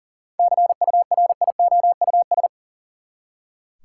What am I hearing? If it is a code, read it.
Morse code, heard as a keyed tone, about 40 wpm, 700 Hz. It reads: CURIOUS
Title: Seven Letter Words - 40wpm